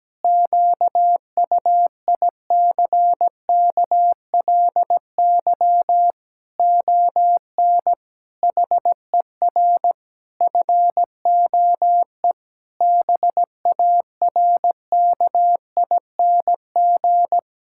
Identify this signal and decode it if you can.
Morse code, heard as a keyed tone, about 17 wpm, 700 Hz. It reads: QUICKLY ON HER FOE BARKING
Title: Pangram Challenge v2 - 17wpm